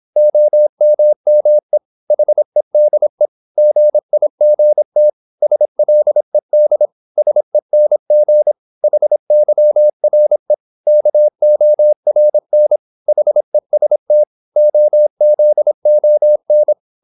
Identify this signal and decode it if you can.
Morse code, heard as a keyed tone, about 26 wpm, 600 Hz. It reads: OMME HEDE GIGT SLED SENG HYRE KORN HEST OZON
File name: Korte ord DK 26wpm